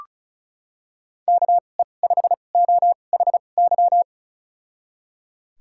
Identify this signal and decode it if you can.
Morse code, heard as a keyed tone, about 36 wpm, spaced as for 25 wpm, 700 Hz. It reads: KE5OHY
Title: USA Call Signs - 6 Characters - 2x Character Spacing - 35wpm